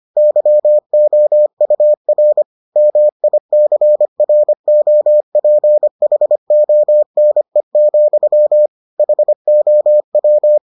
Morse code, heard as a keyed tone, about 25 wpm, 600 Hz. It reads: YOUR MICROPHONE, HOW